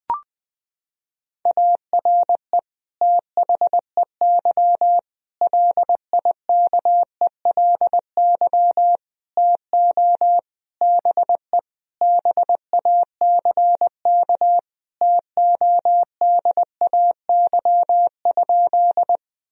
Morse code, heard as a keyed tone, about 20 wpm, 700 Hz. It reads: ARE THEY LIKELY TO BE BACK TODAY?
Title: Sentences from Top 1000 Words - Review Part 4 - 20wpm